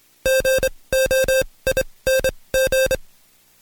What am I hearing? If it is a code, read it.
Morse code, heard as a keyed tone, about 26 wpm, spaced as for 20 wpm, 530 Hz. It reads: GOING
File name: Code Quick / AudioDownld2 / Seg136-Stepper9 19WPM